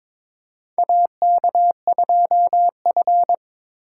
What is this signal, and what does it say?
Morse code, heard as a keyed tone, about 23 wpm, 700 Hz. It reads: AK2F
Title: USA Call Signs - 3x Character Spacing - 22wpm